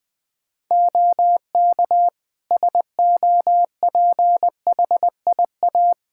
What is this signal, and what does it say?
Morse code, heard as a keyed tone, about 20 wpm, 700 Hz. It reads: OK SOPHIA
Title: Practice QSO - Exchange 3 - More Info - 20wpm